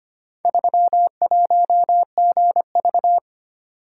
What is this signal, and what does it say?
Morse code, heard as a keyed tone, about 25 wpm, 700 Hz. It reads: 31GV
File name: practice / 25wpm / Four Letter-Number Combinations - 25wpm